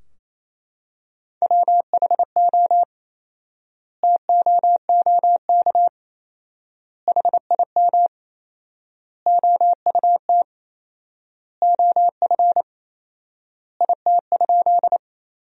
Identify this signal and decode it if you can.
Morse code, heard as a keyed tone, about 28 wpm, 700 Hz. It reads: WHO TOOK HIM OUT OF IT?
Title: 4x Word-Spacing - Sentences from Top 300 Words - 28wpm